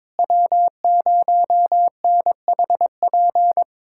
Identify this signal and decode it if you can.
Morse code, heard as a keyed tone, about 22 wpm, 700 Hz. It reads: W0NHP
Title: USA Callsign DE Callsign - 5 Characters - 22wpm